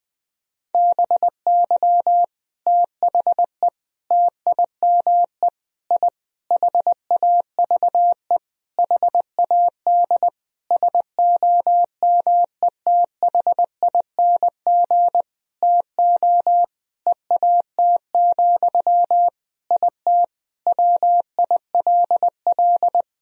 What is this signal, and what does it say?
Morse code, heard as a keyed tone, about 20 wpm, 700 Hz. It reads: BY THE TIME I HAVE HAD SOMETHING TO EAT, IT WILL
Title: Sentences from Top 1000 Words - Review Part 1 - 20wpm